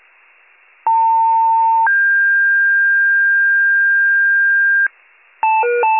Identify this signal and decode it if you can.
Two-tone page, heard as a one-second tone, then a three-second tone, about 900 then 1600 Hz, with noise. It is heard from 0.9 to 4.9 s.